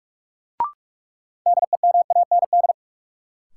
Morse code, heard as a keyed tone, about 42 wpm, 700 Hz. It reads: DEMAND
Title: Sentences from Top 800 Words - New Word Drill - 45wpm